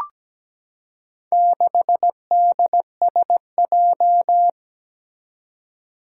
Morse code, heard as a keyed tone, about 17 wpm, 700 Hz. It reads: 6DSJ
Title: Four Challenging Letter-Number Combinations 17wpm